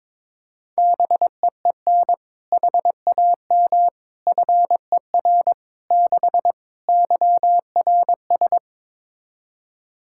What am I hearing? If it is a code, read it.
Morse code, heard as a keyed tone, about 22 wpm, 700 Hz. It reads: BEEN HAM FER 6 YRS
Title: K1USN SST QSO Practice 22wpm